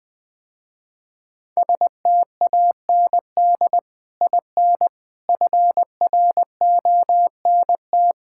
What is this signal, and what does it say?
Morse code, heard as a keyed tone, about 20 wpm, 700 Hz. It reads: STAND IN FRONT
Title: Sentences with 2-Word Phrases - Spoken Beginning - 20wpm